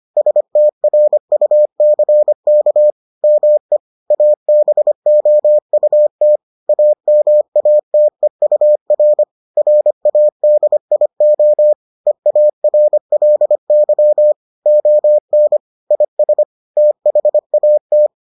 Morse code, heard as a keyed tone, about 25 wpm, 600 Hz. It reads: STRUCK ME ABOUT AMATEUR RADIO EARLY ON IS THAT